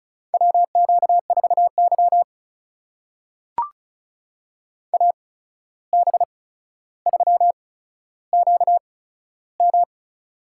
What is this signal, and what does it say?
Morse code, heard as a keyed tone, about 36 wpm, spaced as for 14 wpm, 700 Hz. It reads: WQ4Y AB3QM
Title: USA Call Signs - 8x Character Spacing - 35wpm